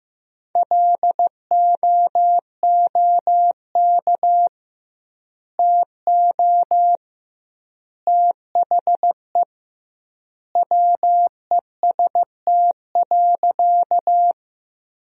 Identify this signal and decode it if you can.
Morse code, heard as a keyed tone, about 15 wpm, 700 Hz. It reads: LOOK TO THE WEST.
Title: 2x Word-Spacing - Sentences from Top 400 Words - 15wpm